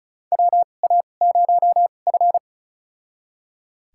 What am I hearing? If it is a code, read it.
Morse code, heard as a keyed tone, about 34 wpm, spaced as for 25 wpm, 700 Hz. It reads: WA0F
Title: USA Call Signs - 4 Characters - 2x Character Spacing - 35wpm